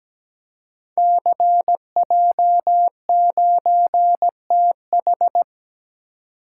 Morse code, heard as a keyed tone, about 17 wpm, 700 Hz. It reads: CJ9TH
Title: Code Groups - 5 Letter:Numbers - 17wpm